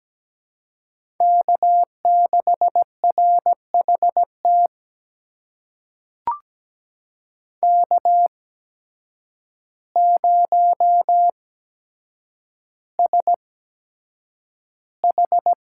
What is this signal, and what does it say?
Morse code, heard as a keyed tone, about 17 wpm, spaced as for 8 wpm, 700 Hz. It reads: K6RHT K0SH
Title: USA Call Signs - 8x Character Spacing - 17wpm